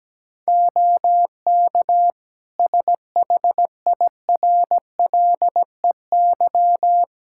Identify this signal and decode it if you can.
Morse code, heard as a keyed tone, about 17 wpm, 700 Hz. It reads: OK SHIRLEY
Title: Practice QSO - Exchange 3 - More Info - 17wpm